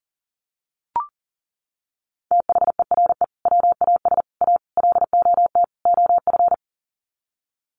Morse code, heard as a keyed tone, about 40 wpm, 700 Hz. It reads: THERE WAS A LOT OF
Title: Sets of 5 Words 40wpm